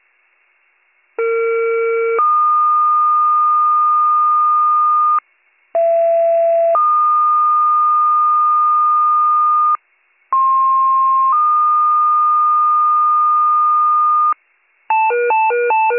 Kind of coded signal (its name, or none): two-tone page